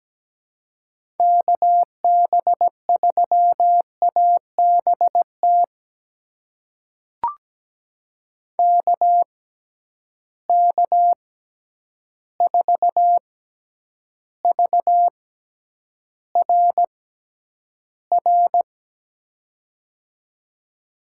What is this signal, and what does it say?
Morse code, heard as a keyed tone, about 17 wpm, spaced as for 9 wpm, 700 Hz. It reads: KB3ABT KK4VRR
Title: USA Call Signs - 6 Characters - 6x Character Spacing - 17wpm